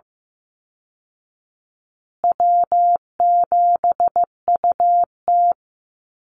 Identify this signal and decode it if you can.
Morse code, heard as a keyed tone, about 15 wpm, 700 Hz. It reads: W7UT
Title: Call Signs Repeated 2x - 15wpm